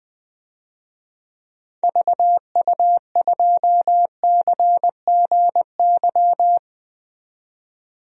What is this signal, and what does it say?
Morse code, heard as a keyed tone, about 20 wpm, 700 Hz. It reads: VU2CGY
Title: DXCC-100 Call Signs - 6 Characters - 20wpm